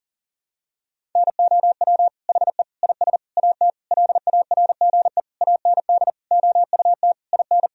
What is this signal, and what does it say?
Morse code, heard as a keyed tone, about 40 wpm, 700 Hz. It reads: NOW HE IS AT LARGE AND OUT IN
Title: Sentences from Top 1000 Words - Review Part 1 - 40wpm